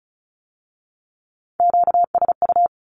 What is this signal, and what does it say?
Morse code, heard as a keyed tone, about 36 wpm, 700 Hz. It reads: QSU
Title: Q-Codes with Spoken Meaning - 35wpm